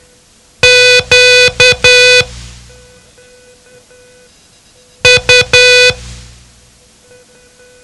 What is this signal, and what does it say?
Morse code, heard as a keyed tone, about 10 wpm, spaced as for 3 wpm, 2500 Hz. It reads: QU
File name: ham radio / CodeQuick / Code Quick 07